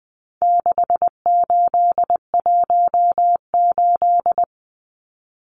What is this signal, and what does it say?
Morse code, heard as a keyed tone, about 20 wpm, 700 Hz. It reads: 6818
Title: Numbers - 4-Digit 20wpm